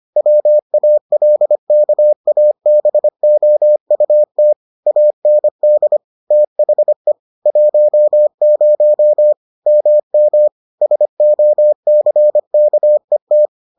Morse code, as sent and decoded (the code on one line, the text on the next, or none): .-- .- .-.. -.- .- -... --- ..- - / .- -. -.. / - .... / .---- ----- / -- -- / ... --- -.-. -.- . -
WALKABOUT AND THE 10 MM SOCKET